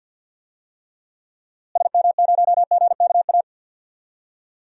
Morse code, heard as a keyed tone, about 45 wpm, 700 Hz. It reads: SM0GKA
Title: DXCC-100 Call Signs - 6 Characters - Repeated 3x - 50wpm